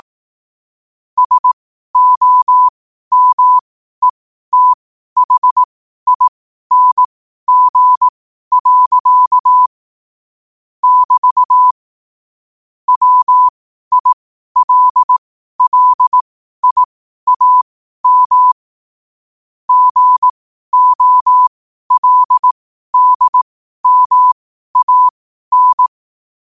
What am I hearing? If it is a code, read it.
Morse code, heard as a keyed tone, about 18 wpm, spaced as for 12 wpm, 1000 Hz. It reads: SOMETHING. <BT> WILLIAM GOLDMAN
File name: qotdfiles / 12wpm